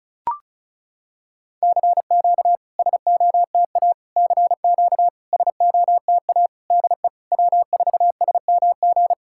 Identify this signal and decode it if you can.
Morse code, heard as a keyed tone, about 36 wpm, 700 Hz. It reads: CQ SOTA CQ SOTA DE W4SMG
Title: SOTA QSO Practice 35wpm